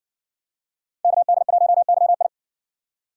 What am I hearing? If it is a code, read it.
Morse code, heard as a keyed tone, about 56 wpm, 700 Hz. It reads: KD1YI